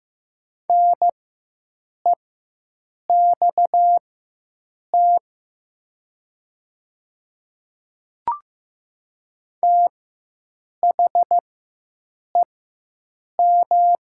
Morse code, heard as a keyed tone, about 15 wpm, spaced as for 7 wpm, 700 Hz. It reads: NEXT THEM
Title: 4x Character Spacing - Top 300 Words - 15wpm